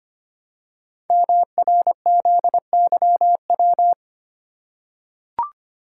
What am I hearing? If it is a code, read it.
Morse code, heard as a keyed tone, about 25 wpm, 700 Hz. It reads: MRZYW
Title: Code Groups - 5 Letters - 25wpm